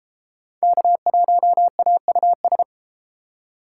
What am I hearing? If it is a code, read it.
Morse code, heard as a keyed tone, about 32 wpm, 700 Hz. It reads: K1AUS